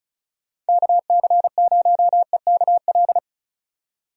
Morse code, heard as a keyed tone, about 34 wpm, 700 Hz. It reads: KC0EKL
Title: USA Call Signs - 6 Characters - 35wpm